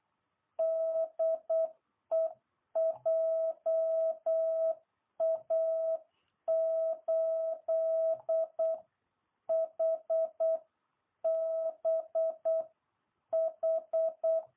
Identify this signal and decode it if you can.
Morse code, heard as a keyed tone, about 8 wpm, 660 Hz. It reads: DE JA8HBH